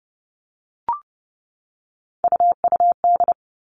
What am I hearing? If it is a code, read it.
Morse code, heard as a keyed tone, about 30 wpm, 700 Hz. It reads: UUD
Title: All Cut Numbers - 3-Digit 30wpm